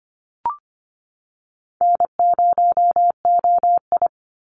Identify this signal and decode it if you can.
Morse code, heard as a keyed tone, about 25 wpm, 690 Hz. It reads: N0OI
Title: NA SOTA Activator Callsigns 25wpm